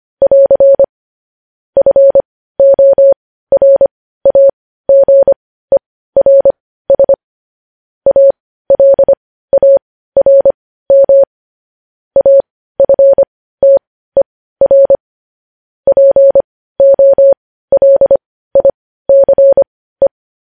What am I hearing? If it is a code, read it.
Morse code, heard as a keyed tone, about 25 wpm, spaced as for 15 wpm, 550 Hz. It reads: <AR> FORAGERS ALARM AFTER POLICE
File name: podcast / News headlines-1x-25x15